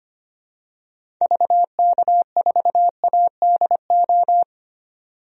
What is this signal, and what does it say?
Morse code, heard as a keyed tone, about 25 wpm, 700 Hz. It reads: VK4ADO
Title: DXCC-100 Call Signs - 6 Characters - 25wpm